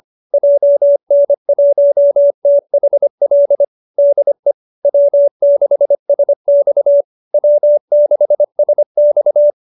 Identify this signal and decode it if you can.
Morse code, heard as a keyed tone, about 25 wpm, 590 Hz. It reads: JN1THL DE W6SX W6SX
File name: QSO305 25